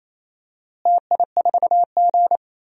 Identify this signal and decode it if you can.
Morse code, heard as a keyed tone, about 28 wpm, 700 Hz. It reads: TI4G
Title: DXCC-100 Call Signs - 4 Characters - Repeated 2x - 28wpm